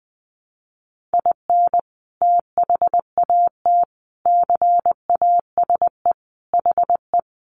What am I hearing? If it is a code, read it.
Morse code, heard as a keyed tone, about 20 wpm, 700 Hz. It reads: IN THAT CASE HE
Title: Sentences from Top 800 Words 20wpm